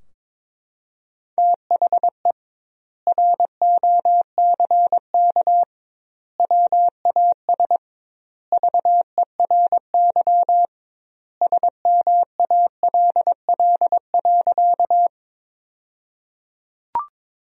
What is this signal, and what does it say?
Morse code, heard as a keyed tone, about 22 wpm, 700 Hz. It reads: THE ROCK WAS VERY SMALL.
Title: 2x Word-Spacing - Sentences from Top 400 Words - 22wpm